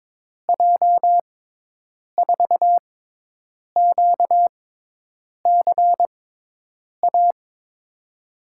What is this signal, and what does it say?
Morse code, heard as a keyed tone, about 23 wpm, spaced as for 8 wpm, 700 Hz. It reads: J4QCA
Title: USA Call Signs - 6 Characters - 6x Character Spacing - 22wpm